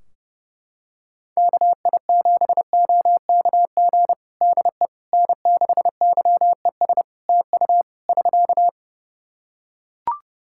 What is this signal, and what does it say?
Morse code, heard as a keyed tone, about 30 wpm, 700 Hz. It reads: KI7OKG DE N6YES TU <SK>
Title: Practice QSO - Exchange 4 - Ending - 30wpm